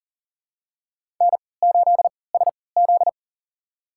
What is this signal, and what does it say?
Morse code, heard as a keyed tone, about 40 wpm, spaced as for 23 wpm, 700 Hz. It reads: N8SZ
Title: USA Call Signs - 4 Characters - 3x Character Spacing - 40wpm